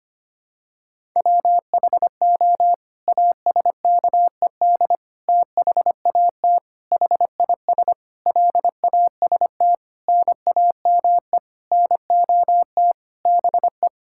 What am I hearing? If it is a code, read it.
Morse code, heard as a keyed tone, about 25 wpm, 700 Hz. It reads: WHO ASKED THAT HIS LAST NAME NOT BE
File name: Sets of 9 Words - 25wpm